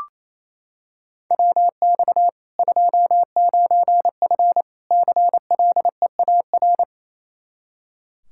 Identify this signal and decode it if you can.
Morse code, heard as a keyed tone, about 28 wpm, 700 Hz. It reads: WX 29F CLEAR
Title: Practice QSO - Exchange 2 - Info - 28wpm